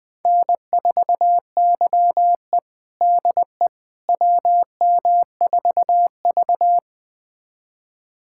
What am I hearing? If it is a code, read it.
Morse code, heard as a keyed tone, about 21 wpm, 700 Hz. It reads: N4YE DE WM4V
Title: USA Callsign DE Callsign - 4 Characters - 20wpm